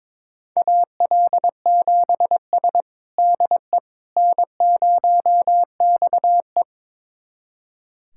Morse code, heard as a keyed tone, about 22 wpm, 700 Hz. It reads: AL7S DE N0XE
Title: USA Callsign DE Callsign - 4 Characters - 22wpm